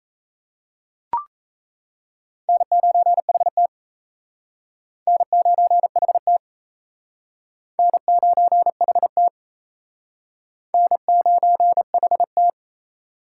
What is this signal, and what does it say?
Morse code, heard as a keyed tone, about 34 wpm, 700 Hz. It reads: N9HT N9HT N9HT N9HT
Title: Speed Racing - USA 1x2 Call Signs 28wpm